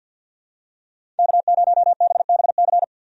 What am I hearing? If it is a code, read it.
Morse code, heard as a keyed tone, about 48 wpm, 700 Hz. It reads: K0BBC
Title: USA Call Signs - 5 Characters - 50wpm